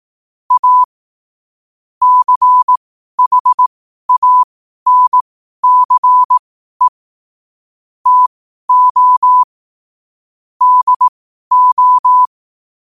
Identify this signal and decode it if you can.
Morse code, heard as a keyed tone, about 18 wpm, spaced as for 12 wpm, 1000 Hz. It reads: A CHANCE TO DO
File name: qotdfiles / 12wpm